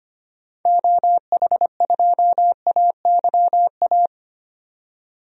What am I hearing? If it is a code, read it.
Morse code, heard as a keyed tone, about 25 wpm, 700 Hz. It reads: OH2AYA
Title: DXCC-100 Call Signs - 6 Characters - 25wpm